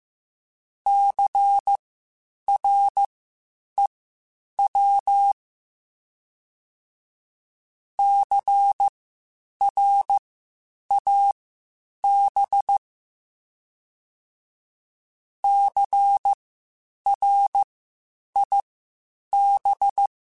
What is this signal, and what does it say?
Morse code, heard as a keyed tone, about 15 wpm, spaced as for 8 wpm, 800 Hz. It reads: CREW CRAB CRIB
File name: Spaced Alphas / Spaced Alphas mp3s / Cs sp @13